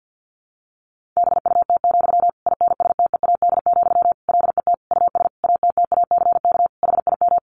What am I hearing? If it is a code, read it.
Morse code, heard as a keyed tone, about 50 wpm, 700 Hz. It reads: BUT, INSTEAD, LET US ATTACK HIM
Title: Sentences from Top 1500 Words 50wpm